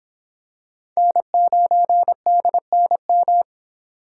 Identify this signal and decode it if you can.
Morse code, heard as a keyed tone, about 25 wpm, 700 Hz. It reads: N9DNM